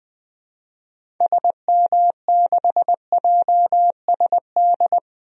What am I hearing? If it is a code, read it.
Morse code, heard as a keyed tone, about 20 wpm, 700 Hz. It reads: SM6JSD
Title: DXCC-100 Call Signs - 6 Characters - 20wpm